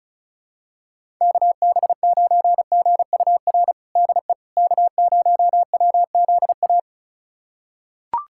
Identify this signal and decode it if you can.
Morse code, heard as a keyed tone, about 34 wpm, 700 Hz. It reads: KB9GUR DE K0WZA
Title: USA Callsign DE Callsign 35wpm